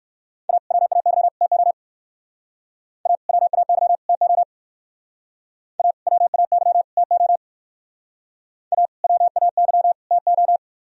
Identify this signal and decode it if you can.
Morse code, heard as a keyed tone, about 59 wpm, 700 Hz. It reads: A WAY TO A WAY TO A WAY TO A WAY TO